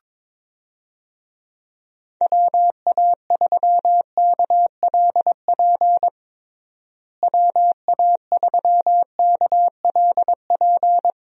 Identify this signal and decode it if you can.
Morse code, heard as a keyed tone, about 22 wpm, 700 Hz. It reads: WA3KLP WA3KLP
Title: USA Call Signs - 6 Characters - Repeated 2x - 22wpm